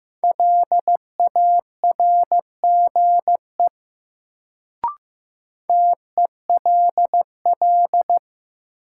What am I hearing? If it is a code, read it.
Morse code, heard as a keyed tone, about 15 wpm, 700 Hz. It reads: LARGE TELL
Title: Sentences from Top 200 Words - New Word Drill - 15wpm